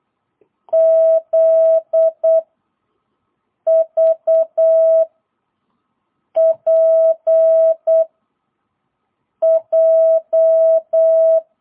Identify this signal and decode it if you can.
Morse code, heard as a keyed tone, about 8 wpm, 660 Hz. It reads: Z V P J
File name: jh8cbh / 2022cw